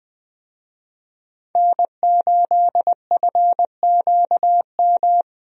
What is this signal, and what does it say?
Morse code, heard as a keyed tone, about 21 wpm, 700 Hz. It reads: N8FQM
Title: USA Call Signs - 5 Characters - Repeated 2x - 20wpm